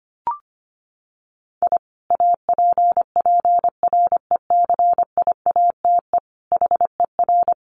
Morse code, heard as a keyed tone, about 25 wpm, 700 Hz. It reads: I APPRECIATE HER
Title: Sentences from Top 1300 Words 25wpm